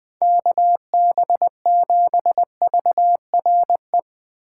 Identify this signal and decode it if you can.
Morse code, heard as a keyed tone, about 21 wpm, 700 Hz. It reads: KB7VRE
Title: USA Call Signs - 6 Characters - 2x Character Spacing - 20wpm